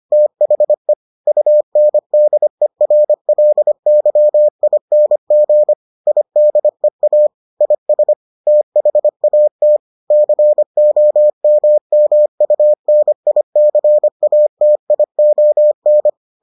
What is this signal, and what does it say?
Morse code, heard as a keyed tone, about 25 wpm, 600 Hz. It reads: THE UNDERLYING IDEA IS THAT COMMUNICATION